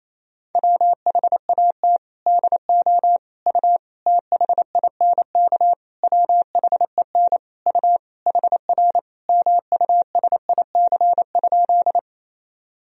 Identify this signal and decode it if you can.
Morse code, heard as a keyed tone, about 28 wpm, 700 Hz. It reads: WHAT DO U THINK WHEN U HR MUSIC?
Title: Sentences from Top 300 Words and Abbreviations 28wpm